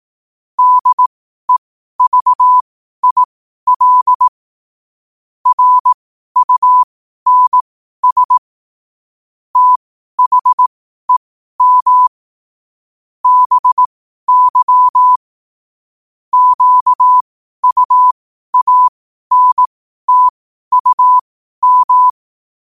Morse code, heard as a keyed tone, about 18 wpm, spaced as for 12 wpm, 1000 Hz. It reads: DEVIL RUNS THEM BY QUANTUM